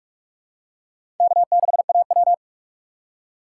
Morse code, heard as a keyed tone, about 47 wpm, 700 Hz. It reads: K6AW
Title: USA Call Signs - 4 Characters - 6x Character Spacing - 45wpm